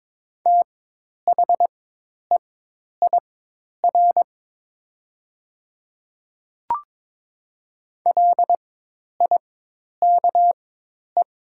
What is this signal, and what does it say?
Morse code, heard as a keyed tone, about 22 wpm, spaced as for 10 wpm, 700 Hz. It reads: THEIR LIKE
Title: 4x Character Spacing - Top 200 Words - 22wpm